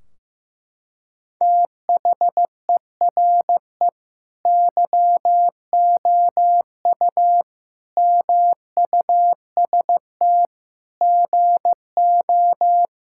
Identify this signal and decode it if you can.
Morse code, heard as a keyed tone, about 15 wpm, 700 Hz. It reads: THERE YOU MUST GO
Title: Sentences from Top 1600 Words 15wpm